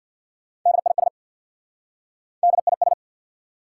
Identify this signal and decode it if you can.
Morse code, heard as a keyed tone, about 59 wpm, 690 Hz. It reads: DIS DIS
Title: Speed-Racing Prefixes and Suffixes 50wpm